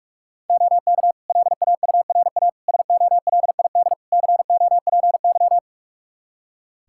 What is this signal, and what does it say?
Morse code, heard as a keyed tone, about 42 wpm, 700 Hz. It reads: OK LAURA SOLID COPY
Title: K1USN SST QSO Practice 45wpm